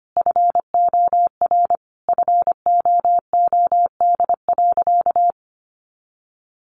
Morse code, heard as a keyed tone, about 25 wpm, 700 Hz. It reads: FOR FOOD.
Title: A Sound Of Thunder 25wpm